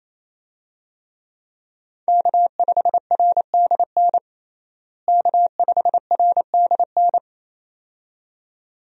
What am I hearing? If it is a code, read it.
Morse code, heard as a keyed tone, about 28 wpm, 700 Hz. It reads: K5RDN K5RDN
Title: USA Call Signs - 5 Characters - Repeated 2x - 28wpm